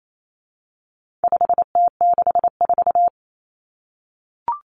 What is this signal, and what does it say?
Morse code, heard as a keyed tone, about 28 wpm, 700 Hz. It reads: 5T64
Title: Common Cut Numbers - 4-Digit 28wpm